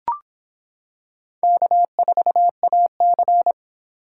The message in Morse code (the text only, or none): K4AC